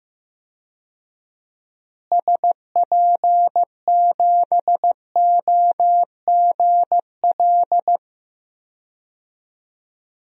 Morse code, heard as a keyed tone, about 15 wpm, 700 Hz. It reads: SP7OGL
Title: DXCC-100 Call Signs - 6 Characters - Repeated 2x - 15wpm